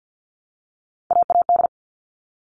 Morse code, heard as a keyed tone, about 50 wpm, 700 Hz. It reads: AAD
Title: All Cut Numbers - 3-Digit 50wpm